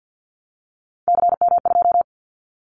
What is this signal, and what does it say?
Morse code, heard as a keyed tone, about 50 wpm, 700 Hz. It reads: CM2
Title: Callsign Prefixes for Top 100 DXCC - 50wpm